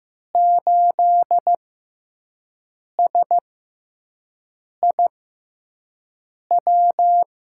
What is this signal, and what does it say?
Morse code, heard as a keyed tone, about 15 wpm, spaced as for 5 wpm, 700 Hz. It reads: 8SIW
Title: USA Call Signs - 6x Character Spacing - 15wpm